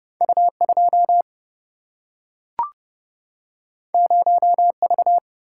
Morse code, heard as a keyed tone, about 30 wpm, 700 Hz. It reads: U2 0V
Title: Single-Letter Single-Number 30wpm